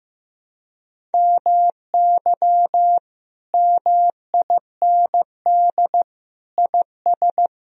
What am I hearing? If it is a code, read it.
Morse code, heard as a keyed tone, about 15 wpm, 700 Hz. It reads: MY MIND IS
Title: Sentences from Top 1000 Words - Review Part 1 - 15wpm